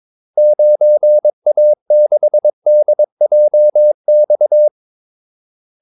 Morse code, heard as a keyed tone, about 22 wpm, 600 Hz. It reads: 9A6DJX
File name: Callsigns-22wpm